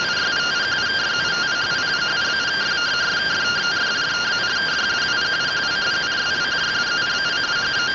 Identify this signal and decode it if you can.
Two-tone teleprinter signal, about 170 Hz shift.